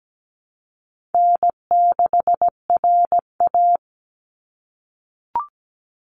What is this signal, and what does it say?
Morse code, heard as a keyed tone, about 17 wpm, 700 Hz. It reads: N6RA
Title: Easy Call Signs - 17wpm